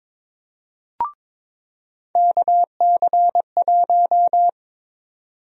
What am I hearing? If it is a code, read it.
Morse code, heard as a keyed tone, about 22 wpm, 700 Hz. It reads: KC1
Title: USA Call Sign Prefixes - 22wpm